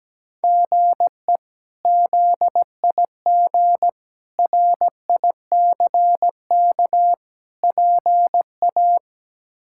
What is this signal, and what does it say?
Morse code, heard as a keyed tone, about 17 wpm, 700 Hz. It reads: GE ZIG RICK PA
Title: K1USN SST QSO Practice - 17wpm